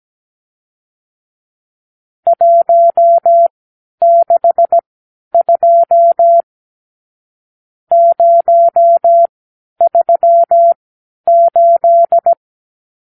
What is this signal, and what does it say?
Morse code, heard as a keyed tone, about 17 wpm, spaced as for 10 wpm, 680 Hz. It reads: 162 038
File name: arrl / Review (1-0)